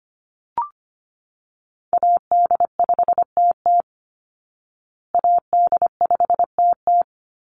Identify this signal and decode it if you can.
Morse code, heard as a keyed tone, about 25 wpm, 700 Hz. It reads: AD5TT AD5TT
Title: Call Signs Repeated 3x - 25wpm